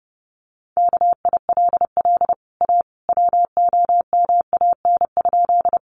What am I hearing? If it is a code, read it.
Morse code, heard as a keyed tone, about 30 wpm, 700 Hz. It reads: KILL A WOMAN?
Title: Sentences from Top 1400 Words 30wpm